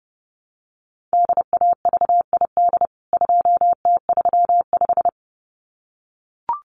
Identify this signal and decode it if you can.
Morse code, heard as a keyed tone, about 30 wpm, 700 Hz. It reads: DAVID 2T35
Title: CWT Exchange 30wpm